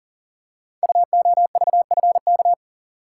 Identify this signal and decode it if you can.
Morse code, heard as a keyed tone, about 38 wpm, 700 Hz. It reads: UOVFK
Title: Code Groups - 5 Letters - 40wpm